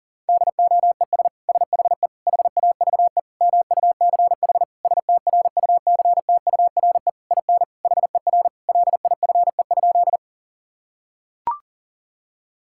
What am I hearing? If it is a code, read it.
Morse code, heard as a keyed tone, about 40 wpm, 700 Hz. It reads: DOES SHE HAVE MUCH STRUCTURE IN HER LIFE?
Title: Sentences from Top 1000 Words - Review Part 6 - 40wpm